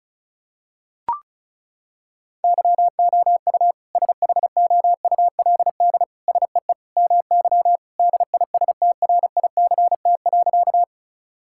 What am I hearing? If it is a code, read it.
Morse code, heard as a keyed tone, about 36 wpm, 700 Hz. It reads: YOU SHOULD SEE MY DISTRICT.
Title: Sentences from Top 1600 Words 35wpm